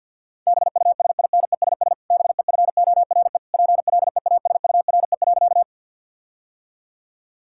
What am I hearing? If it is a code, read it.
Morse code, heard as a keyed tone, about 50 wpm, 700 Hz. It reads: BUSINESS BEFORE PLEASURE.
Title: Sentences from Top 1600 Words 50wpm